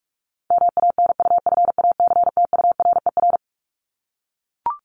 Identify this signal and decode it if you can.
Morse code, heard as a keyed tone, about 44 wpm, 700 Hz. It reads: MANUFACTURER
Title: Top 1751-2000 Words 45wpm